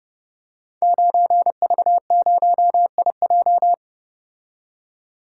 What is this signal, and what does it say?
Morse code, heard as a keyed tone, about 30 wpm, 700 Hz. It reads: 9V0IJ
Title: Code Groups - 5 Letter:Numbers - 30wpm